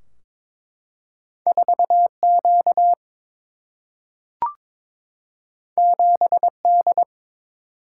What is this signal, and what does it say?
Morse code, heard as a keyed tone, about 22 wpm, 700 Hz. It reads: 4Q 7D
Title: Single-Letter Single-Number 22wpm